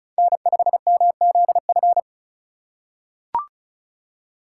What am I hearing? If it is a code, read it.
Morse code, heard as a keyed tone, about 36 wpm, 700 Hz. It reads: N5MZF
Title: USA Call Signs - 5 Characters - 3x Character Spacing - 35wpm